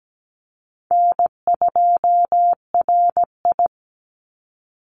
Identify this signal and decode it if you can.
Morse code, heard as a keyed tone, about 17 wpm, 700 Hz. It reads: N2RI
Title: Call Signs Repeated 2x - 17wpm